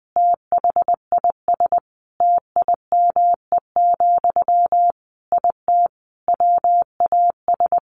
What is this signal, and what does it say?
Morse code, heard as a keyed tone, about 20 wpm, 700 Hz. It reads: THIS TIME, IT WAS
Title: Sentences from Top 1500 Words 20wpm